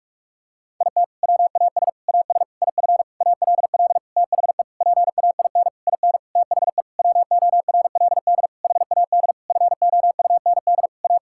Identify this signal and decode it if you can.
Morse code, heard as a keyed tone, about 41 wpm, 700 Hz. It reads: IT WAS AS IF ALL THE PAIN IN THE WORLD HAD FOUND A
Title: Sentences from Top 1600 Words 45wpm